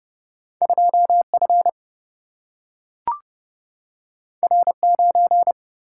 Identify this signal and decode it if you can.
Morse code, heard as a keyed tone, about 30 wpm, 700 Hz. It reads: 2F R9
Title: Single-Letter Single-Number 30wpm